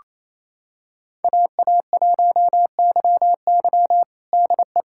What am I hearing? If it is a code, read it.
Morse code, heard as a keyed tone, about 28 wpm, 700 Hz. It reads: AA1YY DE
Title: USA Callsign DE Callsign - 5 Characters - 28wpm